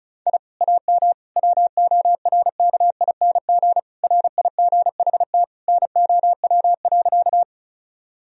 Morse code, heard as a keyed tone, about 36 wpm, 700 Hz. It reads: I AM WORKING RIGHT NOW.
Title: Sentences from Top 1000 Words - Review Part 4 - 35wpm